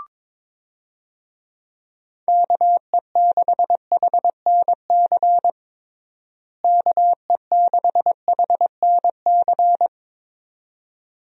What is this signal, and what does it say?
Morse code, heard as a keyed tone, about 21 wpm, 700 Hz. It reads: KE6HNC KE6HNC
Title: USA Call Signs - 6 Characters - Repeated 2x - 22wpm